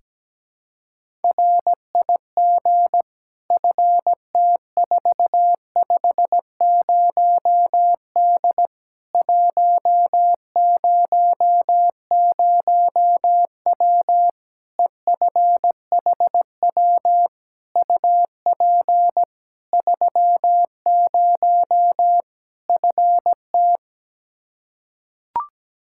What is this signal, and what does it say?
Morse code, heard as a keyed tone, about 17 wpm, 700 Hz. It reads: RIG FT450D 100W EFHW UP 30 FT
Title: K1USN SST QSO Practice 17wpm